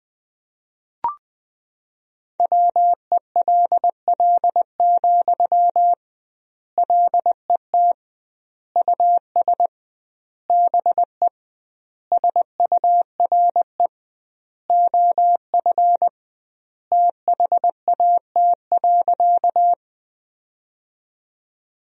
Morse code, heard as a keyed tone, about 21 wpm, 700 Hz. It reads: WELL, LET US BE SURE OF THAT.
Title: 2x Word-Spacing - Sentences from Top 300 Words - 20wpm